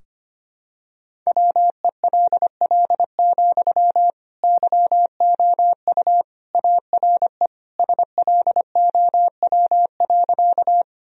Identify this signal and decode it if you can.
Morse code, heard as a keyed tone, about 25 wpm, 700 Hz. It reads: WELL, YOU ARE SLOW.
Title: Sentences from Top 1000 Words - Review Part 2 - 25wpm